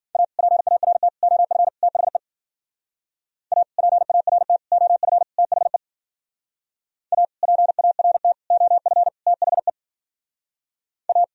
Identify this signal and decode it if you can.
Morse code, heard as a keyed tone, about 51 wpm, 700 Hz. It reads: A PART OF THE A PART OF THE A PART OF THE A